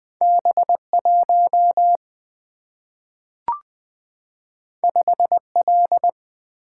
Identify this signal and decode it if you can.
Morse code, heard as a keyed tone, about 20 wpm, 700 Hz. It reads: B1 5L
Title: Single-Letter Single-Number 20wpm